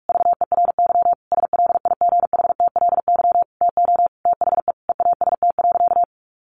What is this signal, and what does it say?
Morse code, heard as a keyed tone, about 44 wpm, 700 Hz. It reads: VERY SLIGHTLY TO THE EAST.
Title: Sentences from Top 1300 Words 45wpm